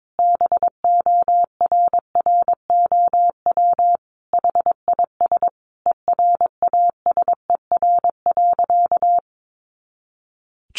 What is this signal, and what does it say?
Morse code, heard as a keyed tone, about 22 wpm, 700 Hz. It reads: BORROW HIS ERASER.
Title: Additional Easy Phrases 22wpm